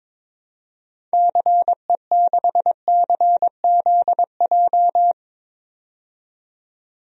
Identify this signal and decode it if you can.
Morse code, heard as a keyed tone, about 22 wpm, 700 Hz. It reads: CE6CZJ
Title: DXCC-100 Call Signs - 6 Characters - 22wpm